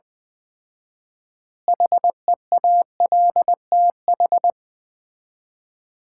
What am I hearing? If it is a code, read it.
Morse code, heard as a keyed tone, about 21 wpm, 700 Hz. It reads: HEALTH
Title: 4x Word-Spacing - Sets of 5 Words - 20wpm